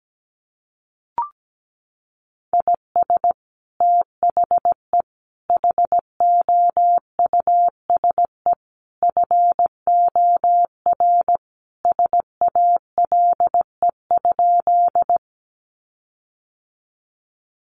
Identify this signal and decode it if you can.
Morse code, heard as a keyed tone, about 17 wpm, 700 Hz. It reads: IS THE HOUSE FOR SALE?
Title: Sentences from Top 1400 Words 17wpm